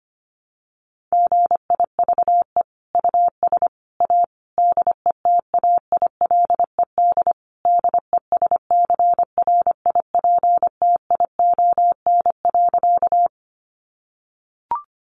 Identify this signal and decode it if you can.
Morse code, heard as a keyed tone, about 25 wpm, 700 Hz. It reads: GIVE US A DETAILED DESCRIPTION.
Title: Sentences from Top 1500 Words 25wpm